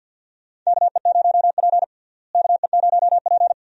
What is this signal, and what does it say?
Morse code, heard as a keyed tone, about 50 wpm, 700 Hz. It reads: KE0P KE0P
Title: USA Call Signs - 4 Characters - Repeated 2x - 50wpm